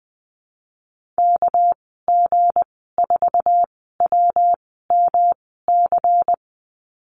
Morse code, heard as a keyed tone, about 20 wpm, spaced as for 15 wpm, 700 Hz. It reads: KG4WMC
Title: 238 Lesson 38g - Letter q - Call Signs 20wpm